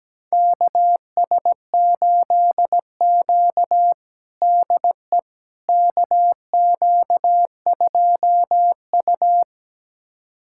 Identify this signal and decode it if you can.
Morse code, heard as a keyed tone, about 17 wpm, 700 Hz. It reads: KS8Q DE KQ2U
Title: USA Callsign DE Callsign - 4 Characters - 17wpm